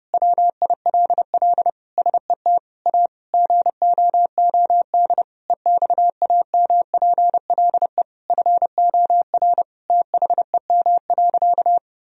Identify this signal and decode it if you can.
Morse code, heard as a keyed tone, about 30 wpm, 700 Hz. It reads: WILL SET A GOOD EXAMPLE FOR THEM.
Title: Sentences from Top 1000 Words - Review Part 1 - 30wpm